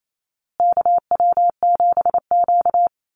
Morse code, heard as a keyed tone, about 28 wpm, 700 Hz. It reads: KW7Q
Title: Call Signs Repeated 2x - 28wpm